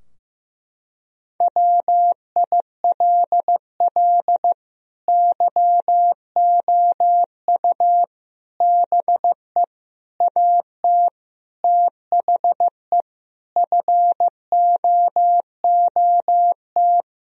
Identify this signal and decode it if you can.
Morse code, heard as a keyed tone, about 15 wpm, 700 Hz. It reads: WILL YOU BE AT THE FOOT
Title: Sentences from Top 1000 Words - Review Part 3 - 15wpm